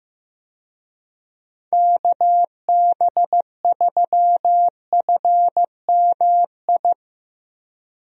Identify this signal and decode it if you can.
Morse code, heard as a keyed tone, about 16 wpm, 700 Hz. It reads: KB3FMI
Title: USA Call Signs - 6 Characters - Repeated 2x - 15wpm